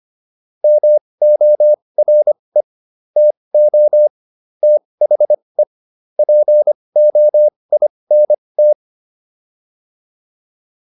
Morse code, heard as a keyed tone, about 25 wpm, spaced as for 20 wpm, 600 Hz. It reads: MORE TO THE POINT